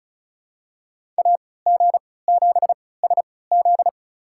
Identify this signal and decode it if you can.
Morse code, heard as a keyed tone, about 34 wpm, spaced as for 20 wpm, 700 Hz. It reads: AG7SZ
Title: USA Call Signs - 5 Characters - 3x Character Spacing - 35wpm